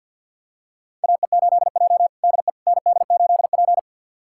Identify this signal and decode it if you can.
Morse code, heard as a keyed tone, about 51 wpm, 700 Hz. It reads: AE8J DE ND8P
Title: USA Callsign DE Callsign - 4 Characters - 50wpm